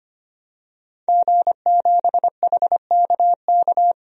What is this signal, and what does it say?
Morse code, heard as a keyed tone, about 25 wpm, 700 Hz. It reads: G7HKK
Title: DXCC-100 Call Signs - 5 Characters - 25wpm